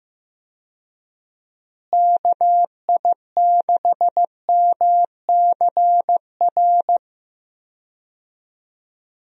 Morse code, heard as a keyed tone, about 15 wpm, 700 Hz. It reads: KI6MCR
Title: USA Call Signs - 6 Characters - Repeated 2x - 15wpm